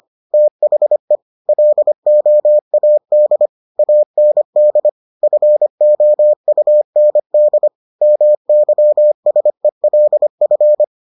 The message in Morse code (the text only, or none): THE LOAD AND FOUND MYSELF